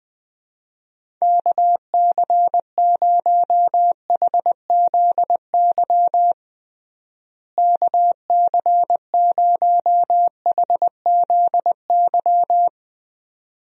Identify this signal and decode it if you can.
Morse code, heard as a keyed tone, about 21 wpm, 700 Hz. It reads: KC0HZY KC0HZY
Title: USA Call Signs - 6 Characters - Repeated 2x - 20wpm